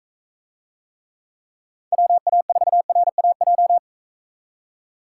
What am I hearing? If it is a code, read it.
Morse code, heard as a keyed tone, about 40 wpm, 700 Hz. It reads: WA4RAJ